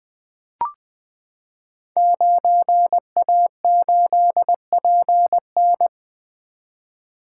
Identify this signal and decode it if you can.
Morse code, heard as a keyed tone, about 20 wpm, 700 Hz. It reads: 9A8PN
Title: Code Groups - 5 Letter:Numbers - 20wpm